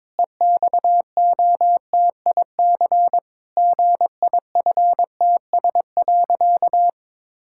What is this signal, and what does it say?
Morse code, heard as a keyed tone, about 22 wpm, 700 Hz. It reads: EXOTIC GIFTS.
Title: Pangram Challenge v2 - 22wpm